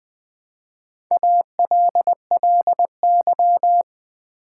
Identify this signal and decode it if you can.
Morse code, heard as a keyed tone, about 20 wpm, 700 Hz. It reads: ALLY
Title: Build A Word - 1 Letter At A Time - 4-Letter Words - 20wpm